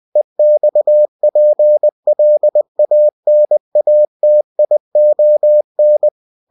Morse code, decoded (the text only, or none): EXPLANATION